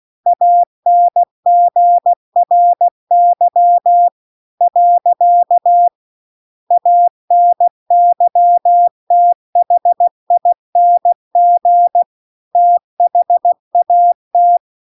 Morse code, heard as a keyed tone, about 16 wpm, 700 Hz. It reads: ANGRY . ANYTHING THAT